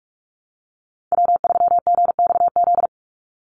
Morse code, heard as a keyed tone, about 44 wpm, 690 Hz. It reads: W3GXZ
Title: USA Call Signs 45wpm